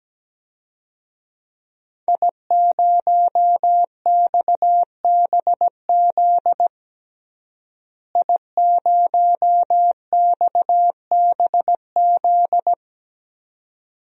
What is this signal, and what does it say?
Morse code, heard as a keyed tone, about 17 wpm, 700 Hz. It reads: I0XBZ I0XBZ
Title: DXCC-100 Call Signs - 5 Characters - Repeated 2x - 17wpm